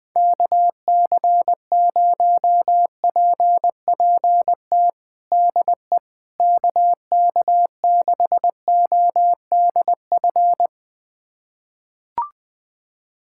Morse code, heard as a keyed tone, about 21 wpm, 700 Hz. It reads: KC0PPT DE KK6ODF
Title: USA Callsign DE Callsign 20wpm